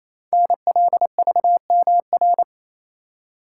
Morse code, heard as a keyed tone, about 29 wpm, 700 Hz. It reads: NLVMR
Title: Code Groups - 5 Letters - 28wpm